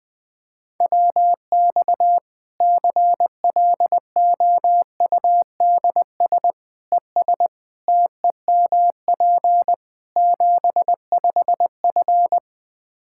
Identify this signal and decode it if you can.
Morse code, heard as a keyed tone, about 20 wpm, 700 Hz. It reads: WX CLOUDS ES TEMP 75F
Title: Practice QSO - Exchange 2 - Info - 20wpm